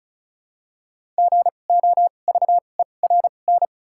Morse code, heard as a keyed tone, about 36 wpm, 700 Hz. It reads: G O V E R N
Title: 2x Character Spacing - Top 500 Words - 35wpm